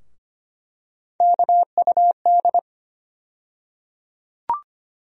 Morse code, heard as a keyed tone, about 25 wpm, 700 Hz. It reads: KUD
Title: USA Call Sign Postfixes - 3 Letters 25wpm